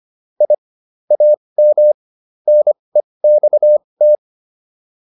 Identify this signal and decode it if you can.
Morse code, heard as a keyed tone, about 25 wpm, spaced as for 20 wpm, 600 Hz. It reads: I AM NEXT